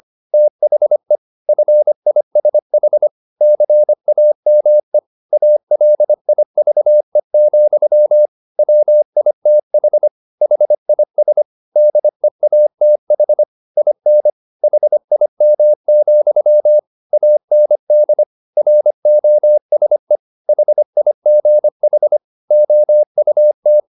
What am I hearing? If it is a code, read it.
Morse code, heard as a keyed tone, about 25 wpm, 600 Hz. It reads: THE FISH CAME ALIVE, WITH HIS DEATH IN HIM, AND ROSE HIGH OUT